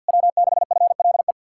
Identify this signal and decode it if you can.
Morse code, heard as a keyed tone, about 56 wpm, 700 Hz. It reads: W6FLE
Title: Speed-Racing Common QSO Phrases Encore - 50wpm